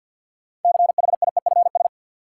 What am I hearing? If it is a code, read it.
Morse code, heard as a keyed tone, about 47 wpm, 700 Hz. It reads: CHIEFS
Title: Sets of 5 Words Encore - New Word Drill - 50wpm